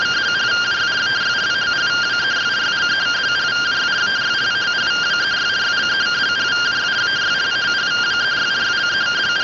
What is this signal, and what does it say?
Two-tone teleprinter signal, about 170 Hz shift.